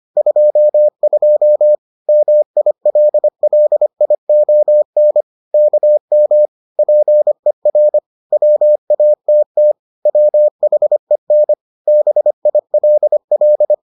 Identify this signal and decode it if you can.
Morse code, heard as a keyed tone, about 25 wpm, 600 Hz. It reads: 22 MILLION KM PER WATT WHEN BILL